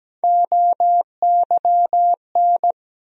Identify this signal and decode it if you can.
Morse code, heard as a keyed tone, about 17 wpm, 700 Hz. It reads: OYN
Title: USA Call Sign Postfixes - 3 Letters 17wpm